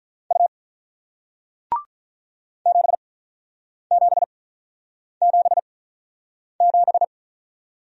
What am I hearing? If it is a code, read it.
Morse code, heard as a keyed tone, about 48 wpm, spaced as for 9 wpm, 700 Hz. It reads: U 7777